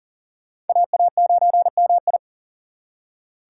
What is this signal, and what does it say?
Morse code, heard as a keyed tone, about 50 wpm, 700 Hz. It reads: AA9MI
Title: Speed Racing - USA 2x2 Call Signs 40wpm